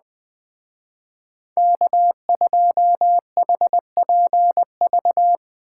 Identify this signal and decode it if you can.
Morse code, heard as a keyed tone, about 20 wpm, 700 Hz. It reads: K2HPV
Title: Speed Racing - USA 1x3 Call Signs 17wpm